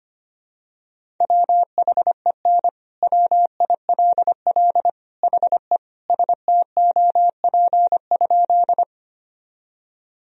Morse code, heard as a keyed tone, about 25 wpm, 700 Hz. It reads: WHEN WILL HE STOP?
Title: Sentences from Top 1000 Words - Review Part 1 - 25wpm